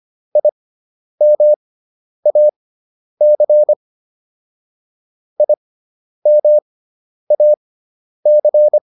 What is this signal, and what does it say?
Morse code, heard as a keyed tone, about 25 wpm, spaced as for 10 wpm, 600 Hz. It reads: IMAC IMAC
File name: IM4 10